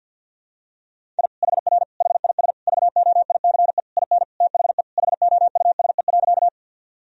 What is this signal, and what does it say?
Morse code, heard as a keyed tone, about 52 wpm, 700 Hz. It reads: I HR HIS VOICE IN THE HOUSE.
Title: Sentences from Top 500 Words and Abbreviations 50wpm